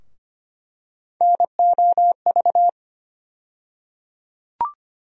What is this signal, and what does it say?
Morse code, heard as a keyed tone, about 25 wpm, 700 Hz. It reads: NOV
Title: USA Call Sign Postfixes - 3 Letters 25wpm